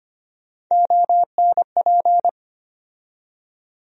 Morse code, heard as a keyed tone, about 25 wpm, 700 Hz. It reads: ONP
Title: USA Call Sign Postfixes - 3 Letters 25wpm